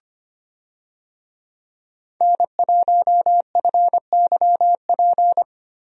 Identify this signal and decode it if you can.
Morse code, heard as a keyed tone, about 25 wpm, 700 Hz. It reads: N1FYP
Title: USA Call Signs - 5 Characters - Repeated 2x - 25wpm